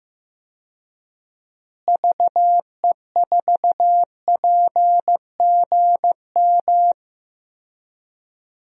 Morse code, heard as a keyed tone, about 15 wpm, 700 Hz. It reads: VE4PGM
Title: DXCC-100 Call Signs - 6 Characters - Repeated 2x - 15wpm